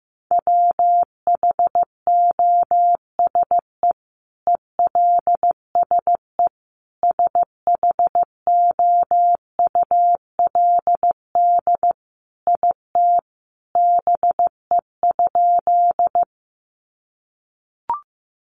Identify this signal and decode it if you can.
Morse code, heard as a keyed tone, about 15 wpm, 700 Hz. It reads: WHOSE ELSE SHOULD IT BE?
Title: Sentences from Top 1200 Words 15wpm